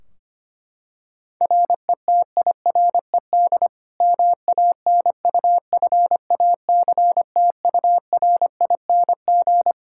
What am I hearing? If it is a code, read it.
Morse code, heard as a keyed tone, about 25 wpm, 700 Hz. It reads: RETIRED MANUFACTURING
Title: Practice QSO - Exchange 3 - More Info - 25wpm